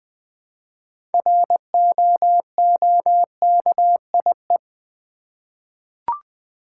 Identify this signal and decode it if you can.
Morse code, heard as a keyed tone, about 20 wpm, 700 Hz. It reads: ROOKIE
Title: 6 Letter Words - First 3 Letters Given - 20wpm